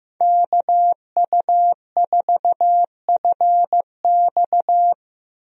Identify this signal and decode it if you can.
Morse code, heard as a keyed tone, about 15 wpm, 700 Hz. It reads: KU4FX
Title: USA Call Signs - 5 Characters - 15wpm